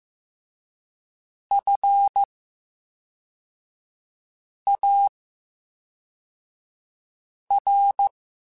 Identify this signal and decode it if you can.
Morse code, heard as a keyed tone, about 15 wpm, 800 Hz. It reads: F A R
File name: Special INGs / Special INGs-mp3s / VerySlowToFaster Ing @10